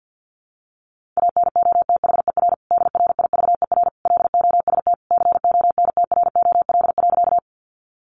Morse code, heard as a keyed tone, about 50 wpm, 700 Hz. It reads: ANOTHER DRIVER LOST CONTROL.
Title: Sentences from Top 1400 Words 50wpm